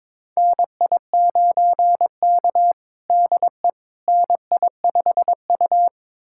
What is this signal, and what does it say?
Morse code, heard as a keyed tone, about 22 wpm, 700 Hz. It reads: NI9K DE NI5U
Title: USA Callsign DE Callsign - 4 Characters - 22wpm